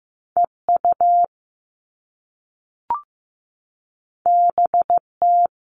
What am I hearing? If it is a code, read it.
Morse code, heard as a keyed tone, about 15 wpm, 700 Hz. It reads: EU BT
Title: All Cut Numbers - 2-Digit 15wpm